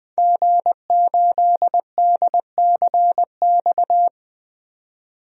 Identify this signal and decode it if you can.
Morse code, heard as a keyed tone, about 20 wpm, 700 Hz. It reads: G8DCX
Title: DXCC-100 Call Signs - 5 Characters - 20wpm